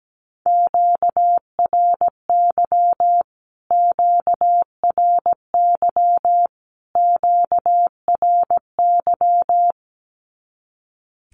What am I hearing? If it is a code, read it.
Morse code, heard as a keyed tone, about 17 wpm, 700 Hz. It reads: QRY QRY QRY
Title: Q-Codes Repeated 3x 17wpm